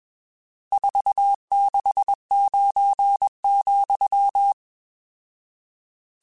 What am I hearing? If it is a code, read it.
Morse code, heard as a keyed tone, about 21 wpm, 800 Hz. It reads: 469,
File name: RSTs / RST-mp3s / Various RST @20